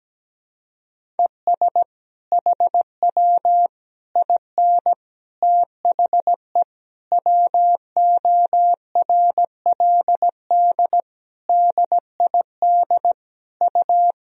Morse code, heard as a keyed tone, about 17 wpm, 700 Hz. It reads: ES HW IN THE WORLD DID U
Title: Sentences from Top 200 Words and Abbreviations 17wpm